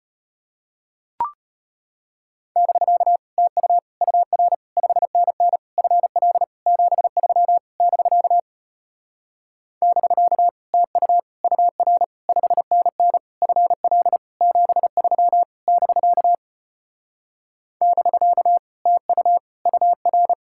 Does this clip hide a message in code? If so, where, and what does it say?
Yes: Morse code from 2.6 to 20.3 s, heard as a keyed tone, about 33 wpm, 700 Hz. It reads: <BK> TU UR 5NN FL 73 <BK> <BK> TU UR 5NN FL 73 <BK> <BK> TU UR